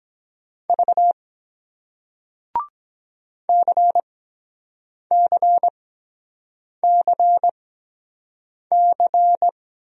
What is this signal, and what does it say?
Morse code, heard as a keyed tone, about 23 wpm, spaced as for 7 wpm, 700 Hz. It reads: V CCCC